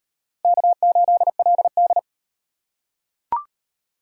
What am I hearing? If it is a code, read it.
Morse code, heard as a keyed tone, about 37 wpm, 700 Hz. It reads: K8LD